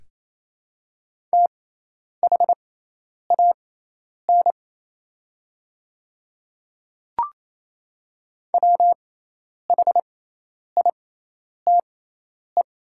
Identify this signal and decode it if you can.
Morse code, heard as a keyed tone, about 28 wpm, spaced as for 10 wpm, 700 Hz. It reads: THAN WHITE
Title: 6x Character Spacing - Top 300 Words - 28wpm